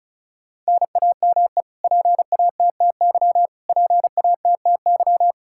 Morse code, heard as a keyed tone, about 36 wpm, 700 Hz. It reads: NAME PATTY PATTY
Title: Practice QSO - Exchange 1 - Beginning - 35wpm